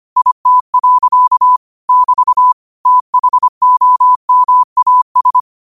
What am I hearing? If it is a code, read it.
Morse code, heard as a keyed tone, about 25 wpm, 1000 Hz. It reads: IT. <BT> THOMAS
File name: qotdfiles / 25wpm